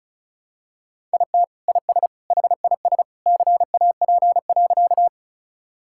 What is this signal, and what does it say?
Morse code, heard as a keyed tone, about 34 wpm, 700 Hz. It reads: IT IS HIS CAP.
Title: Sentences from Top 1600 Words 35wpm